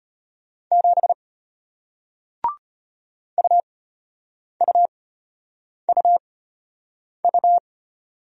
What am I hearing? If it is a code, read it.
Morse code, heard as a keyed tone, about 34 wpm, spaced as for 8 wpm, 700 Hz. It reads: 7 UUUU